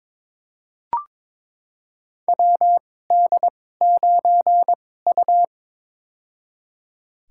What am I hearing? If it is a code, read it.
Morse code, heard as a keyed tone, about 22 wpm, spaced as for 16 wpm, 700 Hz. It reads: WD9U
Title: USA Call Signs - 4 Characters - 2x Character Spacing - 22wpm